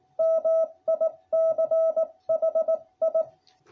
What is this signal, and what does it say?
Morse code, heard as a keyed tone, about 18 wpm, 660 Hz. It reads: MICHI